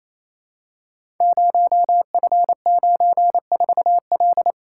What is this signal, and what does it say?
Morse code, heard as a keyed tone, about 29 wpm, 700 Hz. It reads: 0F94L
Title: Code Groups - 5 Letter:Numbers - 28wpm